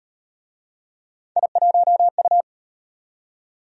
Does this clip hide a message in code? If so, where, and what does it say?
Yes: Morse code from 1.4 to 2.4 s, heard as a keyed tone, about 36 wpm, 700 Hz. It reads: I1U